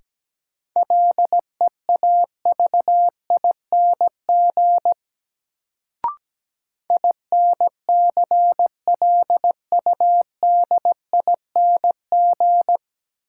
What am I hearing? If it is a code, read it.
Morse code, heard as a keyed tone, about 17 wpm, 700 Hz. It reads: LEAVING INCLUDING
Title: Sentences from Top 1000 Words - Review Part 4 - New Word Drill - 17wpm